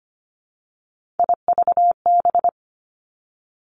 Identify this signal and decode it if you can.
Morse code, heard as a keyed tone, about 25 wpm, 700 Hz. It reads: IVB
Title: Three Challenging Letter-Number Combinations 25wpm